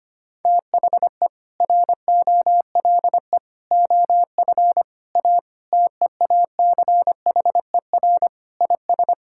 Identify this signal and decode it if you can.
Morse code, heard as a keyed tone, about 25 wpm, 700 Hz. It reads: THE ROLE OF A TEACHER IS
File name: practice / 25wpm / Sentences with 3-Word Phrases - 25wpm